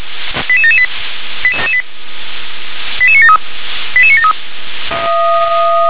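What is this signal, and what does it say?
Five-tone selective call (ZVEI), heard as quick runs of five tones, tones of about 70 ms. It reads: 88778 88973 88973
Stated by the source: ZVEI five-tone selective call received by SDR.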